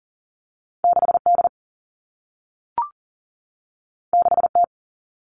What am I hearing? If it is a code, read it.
Morse code, heard as a keyed tone, about 40 wpm, 700 Hz. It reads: 6D 6T
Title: All Cut Numbers - 2-Digit 40wpm